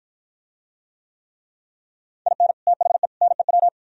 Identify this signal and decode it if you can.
Morse code, heard as a keyed tone, about 49 wpm, 700 Hz. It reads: IN THE NEW